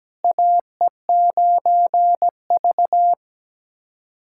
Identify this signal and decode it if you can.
Morse code, heard as a keyed tone, about 17 wpm, 700 Hz. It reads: AE9V
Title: USA Callsign DE Callsign - 4 Characters - 17wpm